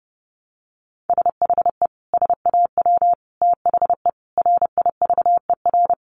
Morse code, heard as a keyed tone, about 30 wpm, 700 Hz. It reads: SHE SAW THE RIVER
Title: Sentences from Top 1200 Words 30wpm